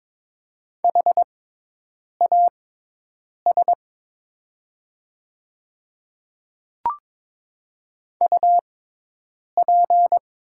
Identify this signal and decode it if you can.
Morse code, heard as a keyed tone, about 22 wpm, spaced as for 8 wpm, 690 Hz. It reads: HAS UP
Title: 6x Character Spacing - Top 300 Words - 22wpm